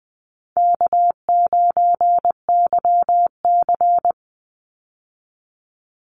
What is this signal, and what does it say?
Morse code, heard as a keyed tone, about 20 wpm, 700 Hz. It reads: K9YC
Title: Call Signs Repeated 3x - 20wpm